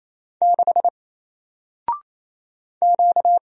Morse code, heard as a keyed tone, about 27 wpm, 700 Hz. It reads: 6 Q
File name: Single-Letter-Number-28wpm